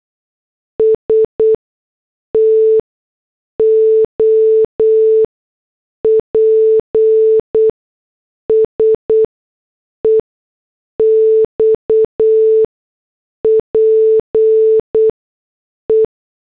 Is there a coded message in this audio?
Yes: Morse code from 0.8 to 16.1 s, heard as a keyed tone, about 8 wpm, spaced as for 6 wpm, 440 Hz. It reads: STOPS EXPE